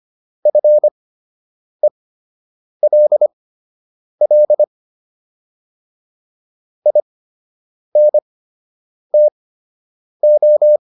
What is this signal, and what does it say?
Morse code, heard as a keyed tone, about 25 wpm, spaced as for 8 wpm, 600 Hz. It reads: FELL INTO